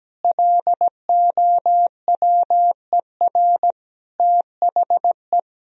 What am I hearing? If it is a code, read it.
Morse code, heard as a keyed tone, about 17 wpm, 700 Hz. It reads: LOWER THE
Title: Sentences from Top 1000 Words - Review Part 4 - 17wpm